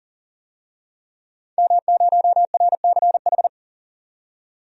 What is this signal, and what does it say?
Morse code, heard as a keyed tone, about 40 wpm, 700 Hz. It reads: M0RCH
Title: DXCC-100 Call Signs - 5 Characters - 40wpm